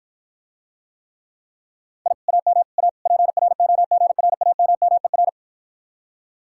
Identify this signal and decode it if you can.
Morse code, heard as a keyed tone, about 54 wpm, 700 Hz. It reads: I AM A PROGRAMMER